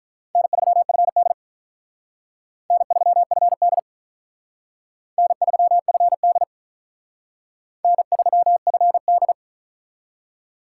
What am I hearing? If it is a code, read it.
Morse code, heard as a keyed tone, about 45 wpm, 700 Hz. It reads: N3FD N3FD N3FD N3FD